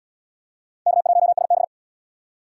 Morse code, heard as a keyed tone, about 72 wpm, 700 Hz. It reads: K1SB